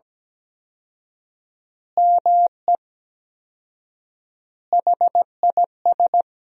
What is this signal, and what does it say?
Morse code, heard as a keyed tone, about 17 wpm, 700 Hz. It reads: ME HIS
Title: 4x Word-Spacing - Sentences from Top 400 Words - 17wpm